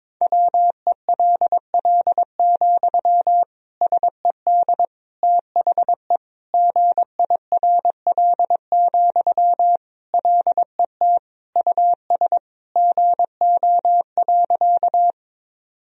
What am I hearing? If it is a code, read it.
Morse code, heard as a keyed tone, about 22 wpm, 700 Hz. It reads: WELL, SED THE GIRL, LET US GO.
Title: Sentences from Top 400 Words and Abbreviations 22wpm